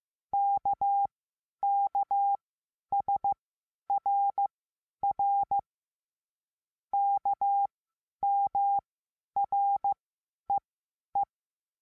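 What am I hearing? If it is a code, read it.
Morse code, heard as a keyed tone, about 15 wpm, spaced as for 10 wpm, 800 Hz. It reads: KKSRR KMREE
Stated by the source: Morse code lesson adding the letter S.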